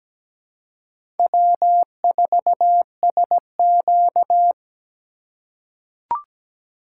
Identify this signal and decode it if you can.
Morse code, heard as a keyed tone, about 17 wpm, 700 Hz. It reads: W4SQ
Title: USA Call Signs - 4 Characters - 4x Character Spacing - 17wpm